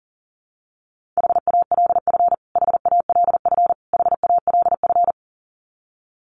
Morse code, heard as a keyed tone, about 40 wpm, 700 Hz. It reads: HALF HALF HALF
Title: Top 500 Words Repeated 40wpm